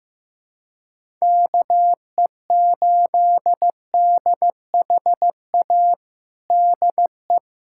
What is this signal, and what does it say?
Morse code, heard as a keyed tone, about 15 wpm, 700 Hz. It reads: KE8DHA DE
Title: USA Callsign DE Callsign - 6 Characters - 15wpm